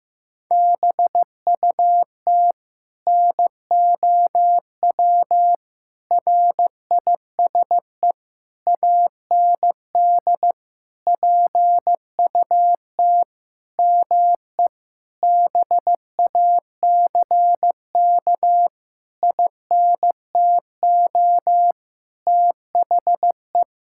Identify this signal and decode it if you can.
Morse code, heard as a keyed tone, about 15 wpm, 700 Hz. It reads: BUT NOW RISE AND PUT ME BACK INTO THE
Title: Sentences from Top 1000 Words - Review Part 6 - 15wpm